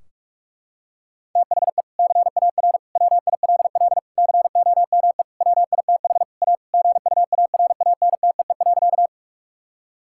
Morse code, heard as a keyed tone, about 47 wpm, 700 Hz. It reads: THE CAR WILL COME WITH A GUARANTEE.
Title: Sentences from Top 1600 Words 45wpm